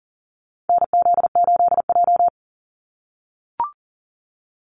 Morse code, heard as a keyed tone, about 38 wpm, 700 Hz. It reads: NZ8J
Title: Call Signs - 40wpm